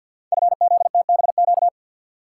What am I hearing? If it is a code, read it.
Morse code, heard as a keyed tone, about 53 wpm, 700 Hz. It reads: FZTBQ
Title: Code Groups - 5 Letters - 50wpm